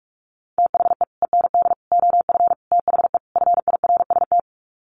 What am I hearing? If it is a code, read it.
Morse code, heard as a keyed tone, about 44 wpm, 700 Hz. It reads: THE END OF THE FIRST
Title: Sets of 5 Words Encore 45wpm